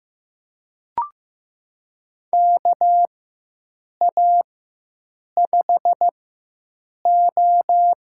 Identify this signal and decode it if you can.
Morse code, heard as a keyed tone, about 15 wpm, spaced as for 7 wpm, 700 Hz. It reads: KA5O
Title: USA Call Signs - 4x Character Spacing - 15wpm